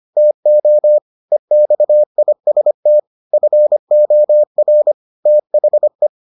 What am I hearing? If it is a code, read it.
Morse code, heard as a keyed tone, about 25 wpm, 600 Hz. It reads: TO EXIST FOR THE